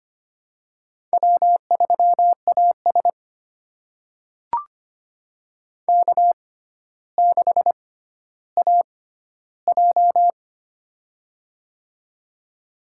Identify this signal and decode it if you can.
Morse code, heard as a keyed tone, about 25 wpm, spaced as for 13 wpm, 700 Hz. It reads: W3AS K6AJ
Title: USA Call Signs - 4 Characters - 6x Character Spacing - 25wpm